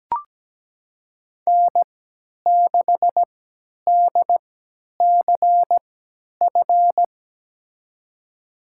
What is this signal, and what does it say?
Morse code, heard as a keyed tone, about 17 wpm, spaced as for 10 wpm, 700 Hz. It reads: N6DCF
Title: USA Call Signs - 5 Characters - 3x Character Spacing - 17wpm